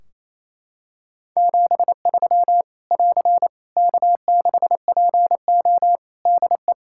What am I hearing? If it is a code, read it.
Morse code, heard as a keyed tone, about 28 wpm, 700 Hz. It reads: 73 <AR> K6PO DE
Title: K1USN SST QSO Practice 28wpm